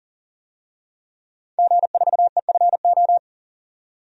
Morse code, heard as a keyed tone, about 40 wpm, 700 Hz. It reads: G4EFO
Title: DXCC-100 Call Signs - 5 Characters - 40wpm